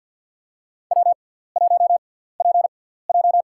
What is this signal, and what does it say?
Morse code, heard as a keyed tone, about 53 wpm, spaced as for 17 wpm, 700 Hz. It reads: W1PJ
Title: USA Call Signs - 4 Characters - 6x Character Spacing - 50wpm